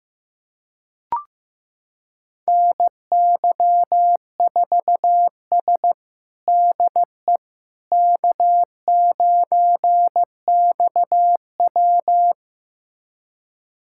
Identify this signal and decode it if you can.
Morse code, heard as a keyed tone, about 15 wpm, 700 Hz. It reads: NY4S DE K9XW
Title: USA Callsign DE Callsign - 4 Characters - 15wpm